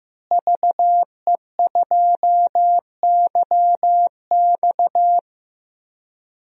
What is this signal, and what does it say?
Morse code, heard as a keyed tone, about 14 wpm, 700 Hz. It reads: VE2YX
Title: DXCC-100 Call Signs - 5 Characters - Repeated 2x - 15wpm